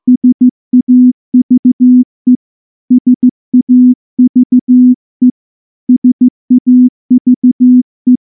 Morse code, heard as a keyed tone, about 15 wpm, 250 Hz. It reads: SAVE SAVE SAVE
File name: morse-training-15WPM